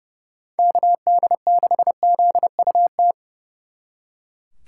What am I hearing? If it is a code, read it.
Morse code, heard as a keyed tone, about 37 wpm, 700 Hz. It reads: KD6ZUT
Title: Speed Racing - USA 2x3 Call Signs 30wpm